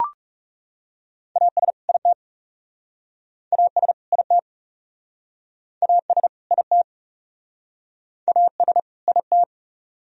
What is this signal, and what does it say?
Morse code, heard as a keyed tone, about 38 wpm, 700 Hz. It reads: AS IT AS IT AS IT AS IT
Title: Speed-Racing - Sets of 2 Words 30wpm